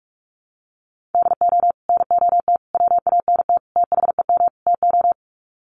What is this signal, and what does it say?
Morse code, heard as a keyed tone, about 44 wpm, 700 Hz. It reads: DO NOT WANT THEM TO
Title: Sets of 5 Words Encore 45wpm